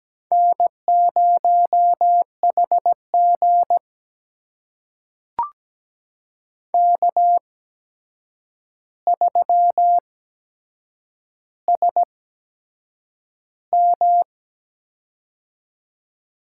Morse code, heard as a keyed tone, about 17 wpm, spaced as for 7 wpm, 700 Hz. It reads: N0HG K3SM
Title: USA Call Signs - 4 Characters - 8x Character Spacing - 17wpm